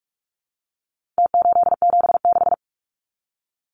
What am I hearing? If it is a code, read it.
Morse code, heard as a keyed tone, about 44 wpm, 700 Hz. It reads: T876
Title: Numbers - Contest Style - 4-Digit 45wpm